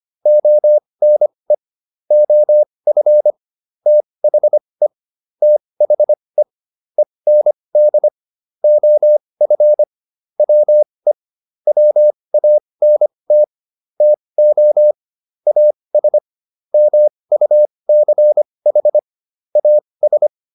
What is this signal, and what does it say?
Morse code, heard as a keyed tone, about 25 wpm, spaced as for 20 wpm, 600 Hz. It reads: ONE OF THE THE END OF WE WANT TO AS MUCH AS
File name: WD204a 20